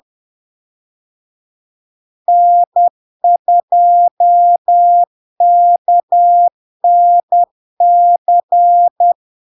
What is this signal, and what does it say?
Morse code, heard as a keyed tone, about 10 wpm, 700 Hz. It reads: N2KNC
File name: LARK-FD-2022-Log-in-CW-10wpm-700-Hz